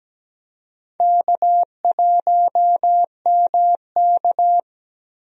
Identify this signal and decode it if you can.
Morse code, heard as a keyed tone, about 17 wpm, 700 Hz. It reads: K1MK
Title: USA Call Signs - 4 Characters - Repeated 3x - 17wpm